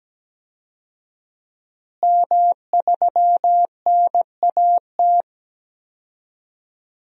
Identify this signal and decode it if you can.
Morse code, heard as a keyed tone, about 17 wpm, 700 Hz. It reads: M3NAT
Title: DXCC-100 Call Signs - 5 Characters - Repeated 2x - 17wpm